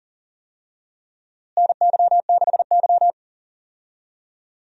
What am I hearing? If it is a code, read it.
Morse code, heard as a keyed tone, about 40 wpm, 700 Hz. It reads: NY6Y
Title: USA Call Signs - 4 Characters - Repeated 3x - 40wpm